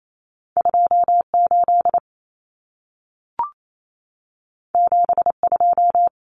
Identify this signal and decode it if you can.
Morse code, heard as a keyed tone, about 28 wpm, 700 Hz. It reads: 28 72
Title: Numbers - 2-Digit 28wpm